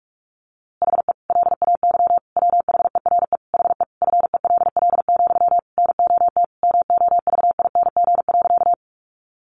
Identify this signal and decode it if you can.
Morse code, heard as a keyed tone, about 44 wpm, 700 Hz. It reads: HE LAY WHERE HE FELL, NOT MOVING.
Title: A Sound Of Thunder 45wpm